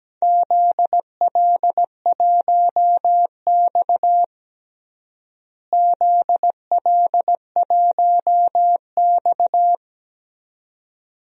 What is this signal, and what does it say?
Morse code, heard as a keyed tone, about 17 wpm, 700 Hz. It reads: ZL1X ZL1X
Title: DXCC-100 Call Signs - 4 Characters - Repeated 2x - 17wpm